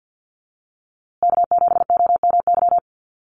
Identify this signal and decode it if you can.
Morse code, heard as a keyed tone, about 48 wpm, 700 Hz. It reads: K7OMY
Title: USA Call Signs Repeated 3x 50wpm